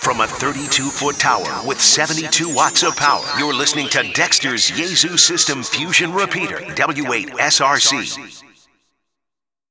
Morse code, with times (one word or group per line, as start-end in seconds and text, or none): none